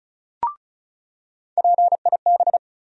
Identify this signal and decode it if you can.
Morse code, heard as a keyed tone, about 36 wpm, 700 Hz. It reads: PIB
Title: USA Call Sign Postfixes - 3 Letters 35wpm